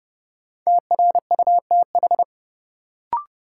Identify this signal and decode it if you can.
Morse code, heard as a keyed tone, about 30 wpm, 700 Hz. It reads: TRUTH
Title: Sentences from Top 800 Words - New Word Drill - 30wpm